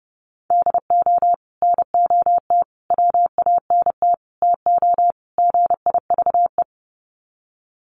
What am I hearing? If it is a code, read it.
Morse code, heard as a keyed tone, about 30 wpm, 700 Hz. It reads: DO NOT WANT TO GIVE
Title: Sets of 5 Words Encore 30wpm